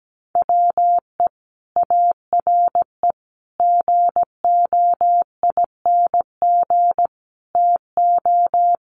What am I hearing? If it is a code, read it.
Morse code, heard as a keyed tone, about 17 wpm, 700 Hz. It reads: WE ARE GOING TO
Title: Sets of 5 Words Encore 17wpm